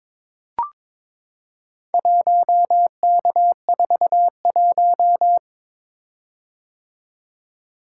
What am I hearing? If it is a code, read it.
Morse code, heard as a keyed tone, about 22 wpm, 700 Hz. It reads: 1K41
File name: practice / 22wpm / Four Letter-Number Combinations - 22wpm